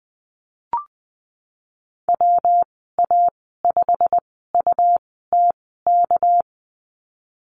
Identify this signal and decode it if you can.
Morse code, heard as a keyed tone, about 20 wpm, spaced as for 15 wpm, 700 Hz. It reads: WA5UTK
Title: 219 Lesson 35g - Number 0 - Call Signs 20wpm